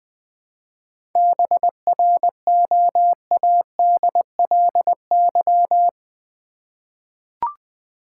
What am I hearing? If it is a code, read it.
Morse code, heard as a keyed tone, about 20 wpm, 700 Hz. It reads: BROADLY
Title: 7 Letter Words - First 3 Letters Given - 20wpm